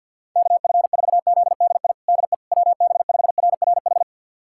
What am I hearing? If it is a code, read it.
Morse code, heard as a keyed tone, about 51 wpm, 700 Hz. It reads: KF4ZDI DE WB5RRH
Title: USA Callsign DE Callsign - 6 Characters - 50wpm